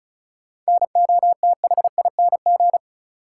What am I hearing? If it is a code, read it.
Morse code, heard as a keyed tone, about 36 wpm, 700 Hz. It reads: NOTHING
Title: Sets of 4 Words - New Word Drill - 35wpm